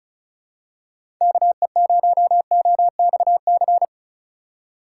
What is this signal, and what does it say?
Morse code, heard as a keyed tone, about 34 wpm, 700 Hz. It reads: KE0OXC
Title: USA Call Signs - 6 Characters - 3x Character Spacing - 35wpm